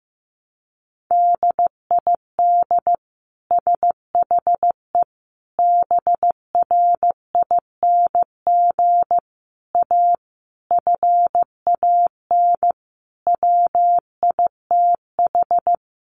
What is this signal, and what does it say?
Morse code, heard as a keyed tone, about 15 wpm, 700 Hz. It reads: DID SHE BRING A FAN WITH
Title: Sentences from Top 1300 Words 15wpm